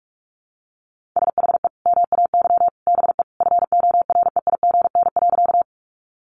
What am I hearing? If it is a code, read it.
Morse code, heard as a keyed tone, about 44 wpm, 700 Hz. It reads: SHE MAY BE FOREIGN.
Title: Sentences from Top 1400 Words 45wpm